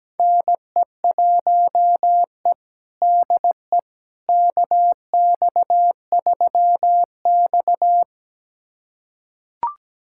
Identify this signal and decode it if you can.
Morse code, heard as a keyed tone, about 17 wpm, 700 Hz. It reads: NE1E DE KX3X
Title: USA Callsign DE Callsign - 4 Characters - 17wpm